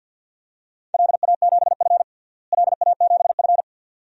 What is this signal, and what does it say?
Morse code, heard as a keyed tone, about 47 wpm, 700 Hz. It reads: LA7F LA7F
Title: DXCC-100 Call Signs - 4 Characters - Repeated 2x - 50wpm